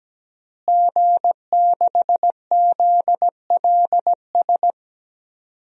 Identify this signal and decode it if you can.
Morse code, heard as a keyed tone, about 17 wpm, 700 Hz. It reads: G6ZLS
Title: DXCC-100 Call Signs - 5 Characters - Repeated 3x - 17wpm